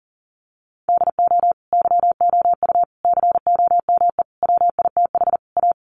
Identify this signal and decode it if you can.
Morse code, heard as a keyed tone, about 40 wpm, 700 Hz. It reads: DO YOU COME WITH A
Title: Sentences from Top 1200 Words 40wpm